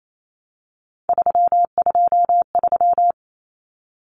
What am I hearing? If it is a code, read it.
Morse code, heard as a keyed tone, about 28 wpm, 700 Hz. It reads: 323
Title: Common Cut Numbers - 3-Digit 28wpm